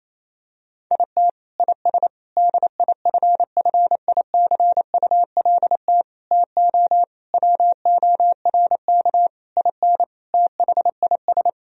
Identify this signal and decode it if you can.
Morse code, heard as a keyed tone, about 28 wpm, 700 Hz. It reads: IT IS DIFFICULT TO WORK IN THIS
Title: Sentences from Top 1000 Words - Review Part 3 - 28wpm